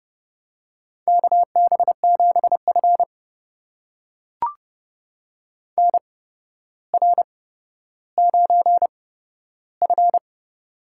Morse code, heard as a keyed tone, about 30 wpm, spaced as for 13 wpm, 700 Hz. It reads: KB7F NR9F
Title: USA Call Signs - 4 Characters - 8x Character Spacing - 30wpm